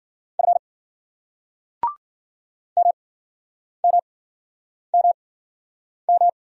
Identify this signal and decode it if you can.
Morse code, heard as a keyed tone, about 58 wpm, spaced as for 9 wpm, 700 Hz. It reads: F MMMM